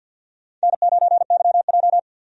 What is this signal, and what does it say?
Morse code, heard as a keyed tone, about 53 wpm, 700 Hz. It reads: N9YJ
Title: USA Call Signs - 4 Characters - 50wpm